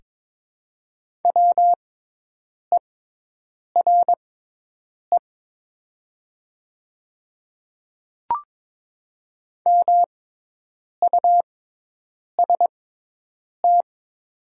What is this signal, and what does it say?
Morse code, heard as a keyed tone, about 22 wpm, spaced as for 8 wpm, 700 Hz. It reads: WERE MUST
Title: 6x Character Spacing - Top 300 Words - 22wpm